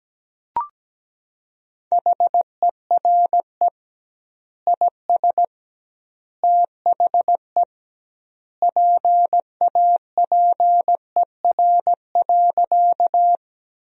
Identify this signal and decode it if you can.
Morse code, heard as a keyed tone, about 17 wpm, 700 Hz. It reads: HERE IS THE PAPER.
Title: 2x Word-Spacing - Sentences from Top 300 Words - 17wpm